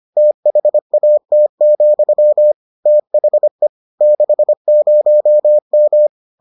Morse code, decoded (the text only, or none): THAT, THE 60M